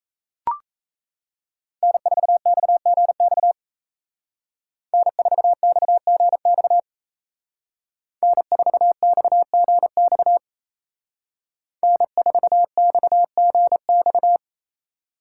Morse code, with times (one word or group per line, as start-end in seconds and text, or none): 1.8-3.5 N4XGX
4.9-6.8 N4XGX
8.2-10.4 N4XGX
11.8-14.4 N4XGX